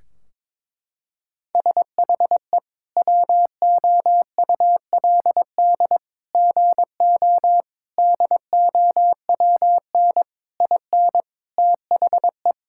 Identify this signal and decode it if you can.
Morse code, heard as a keyed tone, about 22 wpm, 700 Hz. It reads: SHE WOULD GO DOWN IN THE
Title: Sentences from Top 1600 Words 22wpm